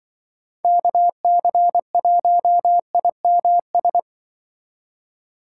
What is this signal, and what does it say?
Morse code, heard as a keyed tone, about 23 wpm, 700 Hz. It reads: KC1IMS
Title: Speed Racing - USA 2x3 Call Signs 20wpm